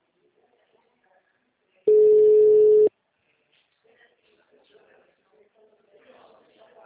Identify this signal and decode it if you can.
Ringback tone (Europe, 425 Hz), with speech.
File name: out- -2006- - - .66